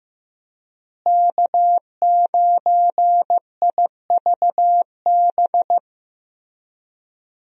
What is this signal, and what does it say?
Morse code, heard as a keyed tone, about 16 wpm, 700 Hz. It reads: K9IVB
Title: USA Call Signs - 5 Characters - Repeated 3x - 15wpm